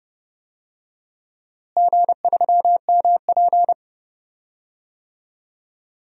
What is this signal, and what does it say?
Morse code, heard as a keyed tone, about 30 wpm, 700 Hz. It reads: G3MP
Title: DXCC-100 Call Signs - 4 Characters - Repeated 2x - 30wpm